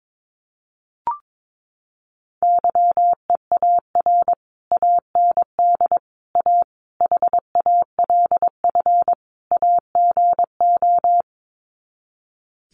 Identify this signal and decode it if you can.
Morse code, heard as a keyed tone, about 22 wpm, 700 Hz. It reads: YEAR AND A HALF AGO
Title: Sets of 5 Words Encore 22wpm